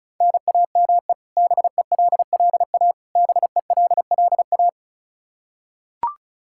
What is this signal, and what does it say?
Morse code, heard as a keyed tone, about 34 wpm, 700 Hz. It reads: NAME BELLA BELLA
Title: Practice QSO - Exchange 1 - Beginning - 35wpm